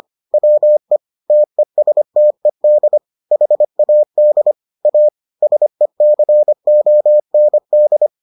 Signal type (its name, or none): Morse code